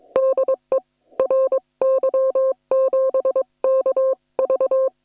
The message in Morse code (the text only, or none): DE RY7KV